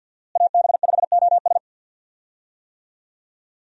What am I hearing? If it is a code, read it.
Morse code, heard as a keyed tone, about 51 wpm, 700 Hz. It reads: AB5OS
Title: USA Call Signs - 5 Characters - Repeated 2x - 50wpm